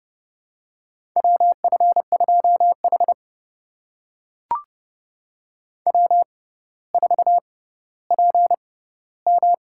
Morse code, heard as a keyed tone, about 30 wpm, spaced as for 15 wpm, 700 Hz. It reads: WF2H W4PM
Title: USA Call Signs - 4 Characters - 6x Character Spacing - 30wpm